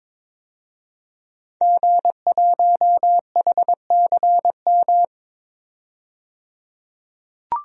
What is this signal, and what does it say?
Morse code, heard as a keyed tone, about 22 wpm, 700 Hz. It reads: G1HCM
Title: DXCC-100 Call Signs - 5 Characters - 22wpm